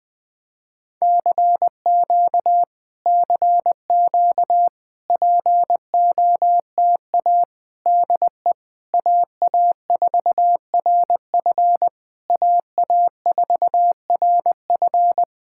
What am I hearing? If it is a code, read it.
Morse code, heard as a keyed tone, about 20 wpm, 700 Hz. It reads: CQ CQ POTA DE AA4RF AA4RF
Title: POTA QSO Practice - 20wpm